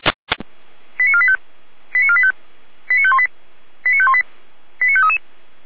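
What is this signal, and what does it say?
Five-tone selective call (ZVEI), heard as quick runs of five tones, tones of about 70 ms. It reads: 89485 89485 89518 89518 89533
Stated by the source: ZVEI five-tone selective call received via SDR.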